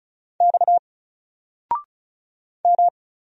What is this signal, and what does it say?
Morse code, heard as a keyed tone, about 36 wpm, 700 Hz. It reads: X M
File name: Single-Letter-Number-35wpm